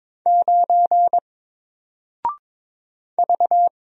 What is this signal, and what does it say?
Morse code, heard as a keyed tone, about 23 wpm, 700 Hz. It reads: 9 V
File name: Single-Letter-Number-22wpm